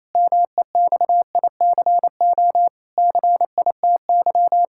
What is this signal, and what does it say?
Morse code, heard as a keyed tone, about 28 wpm, 700 Hz. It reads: MEXICO CITY
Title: Pangram Challenge v2 - 28wpm